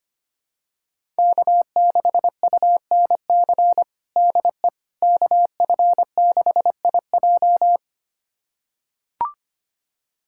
Morse code, heard as a keyed tone, about 25 wpm, 700 Hz. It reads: K6UNC DE KF6IJ
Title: USA Callsign DE Callsign - 5 Characters - 25wpm